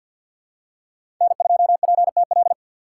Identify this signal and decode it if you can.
Morse code, heard as a keyed tone, about 51 wpm, 700 Hz. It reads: N2PTL
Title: USA Call Signs - 5 Characters - 4x Character Spacing - 50wpm